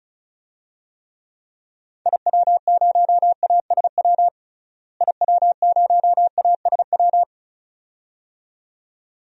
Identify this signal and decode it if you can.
Morse code, heard as a keyed tone, about 36 wpm, 700 Hz. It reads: IW0ASW IW0ASW
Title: DXCC-100 Call Signs - 6 Characters - Repeated 2x - 35wpm